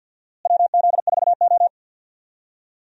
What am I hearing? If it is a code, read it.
Morse code, heard as a keyed tone, about 52 wpm, 700 Hz. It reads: WZ4O
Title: USA Call Signs - 4 Characters - Repeated 2x - 50wpm